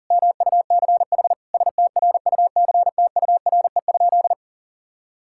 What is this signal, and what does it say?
Morse code, heard as a keyed tone, about 40 wpm, 700 Hz. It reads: MUCH STRUCTURE?
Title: Sentences from Top 1000 Words - Review Part 6 - 40wpm